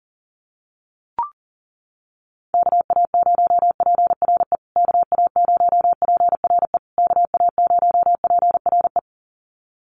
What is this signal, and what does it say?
Morse code, heard as a keyed tone, about 40 wpm, 700 Hz. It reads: KA0PRE KA0PRE KA0PRE
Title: USA Call Signs Repeated 3x 40wpm